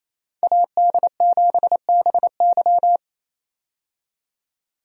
Morse code, heard as a keyed tone, about 28 wpm, 700 Hz. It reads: AD7BY
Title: USA Call Signs - 5 Characters - 28wpm